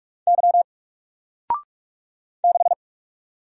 Morse code, heard as a keyed tone, about 46 wpm, 700 Hz. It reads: Y 6
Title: Single Character (Letter:Number:Punctuation:Prosign) 45wpm